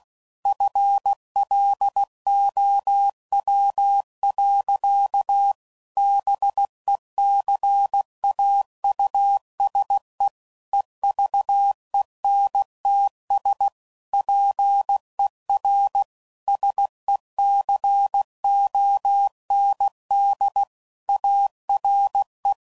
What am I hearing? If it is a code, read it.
Morse code, heard as a keyed tone, about 16 wpm, 800 Hz. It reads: FLOW. BECAUSE EVENTS PER SECOND ARE
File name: CW Code Groups @15WPM / Technical Subjects / XL